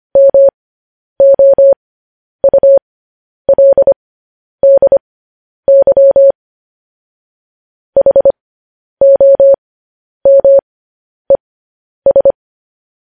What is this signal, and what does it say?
Morse code, heard as a keyed tone, about 25 wpm, spaced as for 10 wpm, 550 Hz. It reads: MOULDY HOMES
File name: podcast / News headlines-1x-25x10